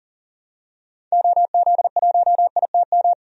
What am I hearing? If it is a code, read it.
Morse code, heard as a keyed tone, about 40 wpm, 700 Hz. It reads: OZ1ITM
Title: DXCC-100 Call Signs - 6 Characters - 40wpm